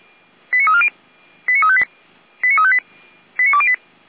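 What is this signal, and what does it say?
Five-tone selective call (ZVEI), heard as quick runs of five tones, tones of about 70 ms. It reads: 89340 89358 89358 89208
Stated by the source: ZVEI five-tone call received on SDR.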